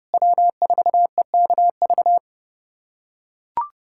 Morse code, heard as a keyed tone, about 30 wpm, 700 Hz. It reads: W4EKV
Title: USA Callsign DE Callsign - 5 Characters - 30wpm